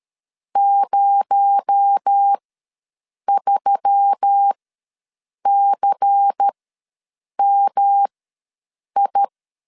Morse code, heard as a keyed tone, about 13 wpm, spaced as for 7 wpm, 800 Hz. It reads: 03CMI
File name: LEZIONI / 6 PKX0 / 6 PKX0 ws 12wpm 3448